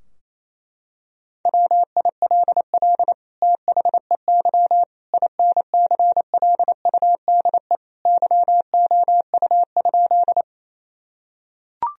Morse code, heard as a keyed tone, about 28 wpm, 700 Hz. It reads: WILL THEY INCLUDE YOU?
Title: Sentences from Top 1000 Words - Review Part 3 - 28wpm